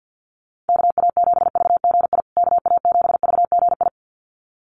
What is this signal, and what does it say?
Morse code, heard as a keyed tone, about 50 wpm, 700 Hz. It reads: KA7VGI KA7VGI
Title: USA Call Signs Repeated 2x 50wpm